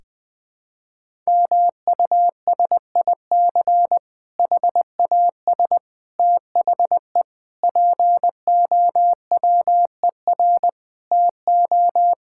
Morse code, heard as a keyed tone, about 20 wpm, 700 Hz. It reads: MUSIC HAS THE POWER TO
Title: Difficult Sentences with Hints - 20wpm